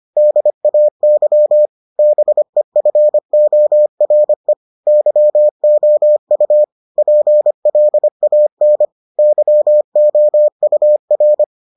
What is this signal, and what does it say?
Morse code, heard as a keyed tone, about 25 wpm, 600 Hz. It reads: DAY BEFORE YOU PLAN YOUR